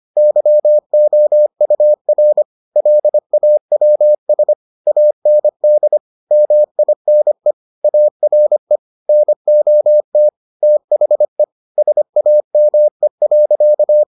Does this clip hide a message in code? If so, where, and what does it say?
Yes: Morse code from 0.2 to 14.0 s, heard as a keyed tone, about 25 wpm, 600 Hz. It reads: YOUR LAWS AND MINE ARE NOT THE SAME.